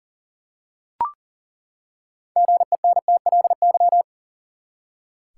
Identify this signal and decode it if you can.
Morse code, heard as a keyed tone, about 40 wpm, 700 Hz. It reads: GENTLY
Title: Six Letter Words - 40wpm